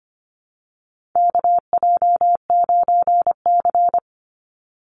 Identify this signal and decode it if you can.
Morse code, heard as a keyed tone, about 25 wpm, 700 Hz. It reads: KJ9C
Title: Call Signs Repeated 3x - 25wpm